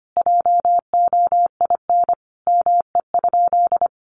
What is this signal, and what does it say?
Morse code, heard as a keyed tone, about 25 wpm, 700 Hz. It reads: JOIN ME?
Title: Sentences from Top 1400 Words 25wpm